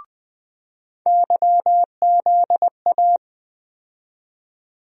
Morse code, heard as a keyed tone, about 20 wpm, 700 Hz. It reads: YZA
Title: USA Call Sign Postfixes - 3 Letters 20wpm